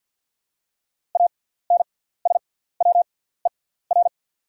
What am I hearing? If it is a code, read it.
Morse code, heard as a keyed tone, about 48 wpm, spaced as for 17 wpm, 700 Hz. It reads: ANSWER
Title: 6x Character Spacing - Top 500 Words - 50wpm